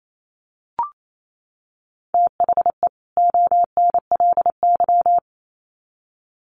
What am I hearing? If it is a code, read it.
Morse code, heard as a keyed tone, about 28 wpm, 700 Hz. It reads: THE ONLY
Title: Sets of 2 Words 28wpm